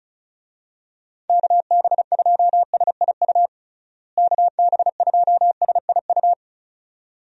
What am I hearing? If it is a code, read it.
Morse code, heard as a keyed tone, about 34 wpm, 700 Hz. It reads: KB2SIU KB2SIU
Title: USA Call Signs - 6 Characters - Repeated 2x - 35wpm